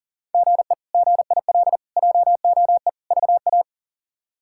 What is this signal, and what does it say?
Morse code, heard as a keyed tone, about 40 wpm, 700 Hz. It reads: GE GIL JOE VA
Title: K1USN SST QSO Practice - 40wpm